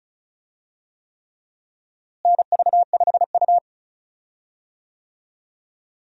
Morse code, heard as a keyed tone, about 36 wpm, 700 Hz. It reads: NV5U
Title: USA Call Signs - 4 Characters - Repeated 3x - 35wpm